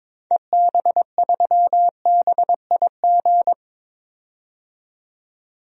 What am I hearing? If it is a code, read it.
Morse code, heard as a keyed tone, about 22 wpm, 700 Hz. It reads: EB3BIG
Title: DXCC-100 Call Signs - 6 Characters - 22wpm